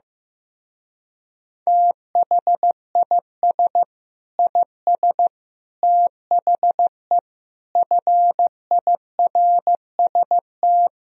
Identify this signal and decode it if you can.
Morse code, heard as a keyed tone, about 16 wpm, 700 Hz. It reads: THIS IS THE FIRST
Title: Speed-Racing - Sets of 5 Words 15wpm